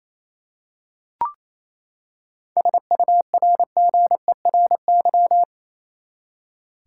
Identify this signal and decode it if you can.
Morse code, heard as a keyed tone, about 28 wpm, 700 Hz. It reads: SURGERY
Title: Seven Letter Words - 28wpm